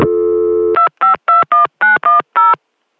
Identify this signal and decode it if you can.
DTMF, heard as telephone touch tones, tones of about 140 ms. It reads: 2521910